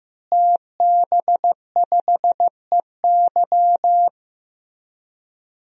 Morse code, heard as a keyed tone, about 15 wpm, 700 Hz. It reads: TB5EY
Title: DXCC-100 Call Signs - 5 Characters - Repeated 3x - 15wpm